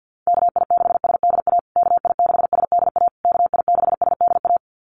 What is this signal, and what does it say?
Morse code, heard as a keyed tone, about 50 wpm, 700 Hz. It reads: KI6SDA KI6SDA KI6SDA
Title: USA Call Signs Repeated 3x 50wpm